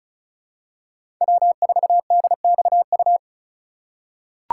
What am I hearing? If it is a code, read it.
Morse code, heard as a keyed tone, about 36 wpm, 700 Hz. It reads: W4DXU
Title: USA Call Signs - 5 Characters - 4x Character Spacing - 35wpm